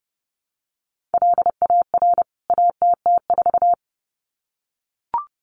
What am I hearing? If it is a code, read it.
Morse code, heard as a keyed tone, about 30 wpm, 700 Hz. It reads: LAR ATT4
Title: CWT Exchange 30wpm